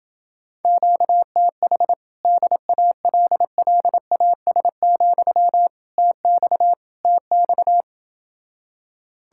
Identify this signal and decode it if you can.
Morse code, heard as a keyed tone, about 27 wpm, 700 Hz. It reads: QTH DALLAS, TX TX